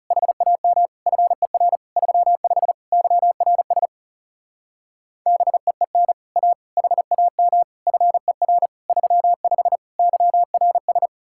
Morse code, heard as a keyed tone, about 37 wpm, 700 Hz. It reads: HAM FER 35 YRS BEEN A HAM FER 35 YRS
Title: Speed-Racing Common QSO Phrases - 30wpm